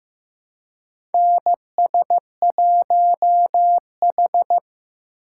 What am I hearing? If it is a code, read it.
Morse code, heard as a keyed tone, about 15 wpm, 700 Hz. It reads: NS1H
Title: USA Call Signs - 4 Characters - 4x Character Spacing - 15wpm